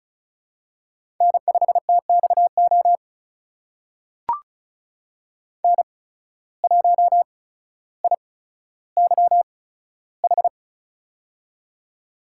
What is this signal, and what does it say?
Morse code, heard as a keyed tone, about 35 wpm, spaced as for 15 wpm, 700 Hz. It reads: N5TXO N1IYH
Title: USA Call Signs - 5 Characters - 8x Character Spacing - 35wpm